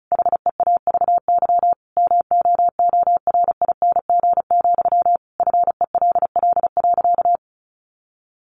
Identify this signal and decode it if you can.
Morse code, heard as a keyed tone, about 34 wpm, 700 Hz. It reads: HEAVY MOORING, FELL.
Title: A Sound Of Thunder 35wpm